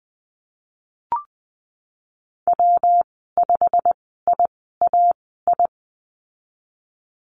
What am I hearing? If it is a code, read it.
Morse code, heard as a keyed tone, about 20 wpm, spaced as for 15 wpm, 700 Hz. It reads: W5IAI
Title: 167 Lesson 27g - Number 7 - Call Signs 20wpm